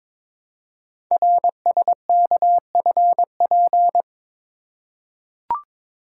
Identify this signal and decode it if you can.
Morse code, heard as a keyed tone, about 23 wpm, 700 Hz. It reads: RSKFP
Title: Code Groups - 5 Letters - 22wpm